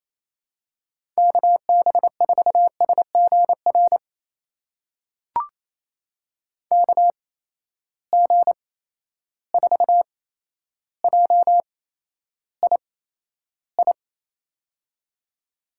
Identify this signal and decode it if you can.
Morse code, heard as a keyed tone, about 28 wpm, spaced as for 12 wpm, 700 Hz. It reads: KB4SGR KG4JII
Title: USA Call Signs - 6 Characters - 8x Character Spacing - 28wpm